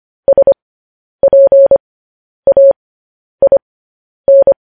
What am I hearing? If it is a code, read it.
Morse code, heard as a keyed tone, about 25 wpm, spaced as for 10 wpm, 550 Hz. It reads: SPAIN
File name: podcast / News headlines-1x-25x10